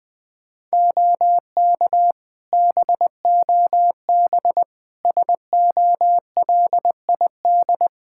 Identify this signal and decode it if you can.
Morse code, heard as a keyed tone, about 20 wpm, 700 Hz. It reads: OK BOB SOLID
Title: Practice QSO - Exchange 3 - More Info - 20wpm